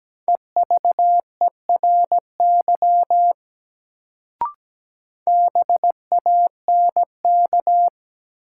Morse code, heard as a keyed tone, about 17 wpm, 700 Hz. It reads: EVERY BANK
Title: Sets of 5 Words - New Word Drill - 17wpm